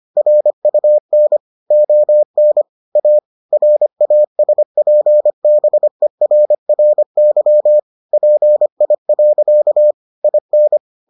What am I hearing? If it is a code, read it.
Morse code, heard as a keyed tone, about 25 wpm, 600 Hz. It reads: RUN ON A RASPBERRY PI. IN